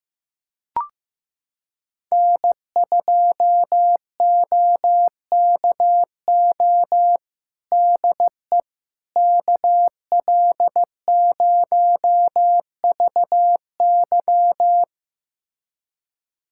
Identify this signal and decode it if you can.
Morse code, heard as a keyed tone, about 16 wpm, 700 Hz. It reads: N2OKO DE KL0VY
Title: USA Callsign DE Callsign 15wpm